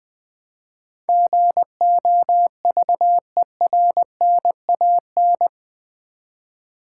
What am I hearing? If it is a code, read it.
Morse code, heard as a keyed tone, about 20 wpm, 700 Hz. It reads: GOVERNAN
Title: Build A Word - 4 Letters At A Time - 10-Letter Words - 20wpm